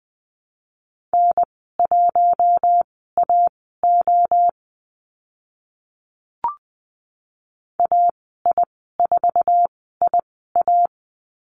Morse code, heard as a keyed tone, about 20 wpm, spaced as for 14 wpm, 700 Hz. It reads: N1AO AI4IA
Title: 048 Lesson 9g - Number 4 - Call Signs 20wpm